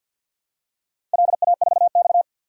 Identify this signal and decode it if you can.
Morse code, heard as a keyed tone, about 48 wpm, 700 Hz. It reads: LA4X
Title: DXCC-100 Call Signs - 4 Characters - 50wpm